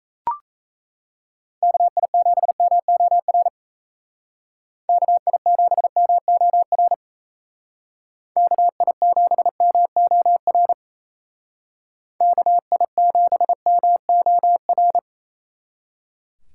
Morse code, heard as a keyed tone, about 37 wpm, 700 Hz. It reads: KI7MOR KI7MOR KI7MOR KI7MOR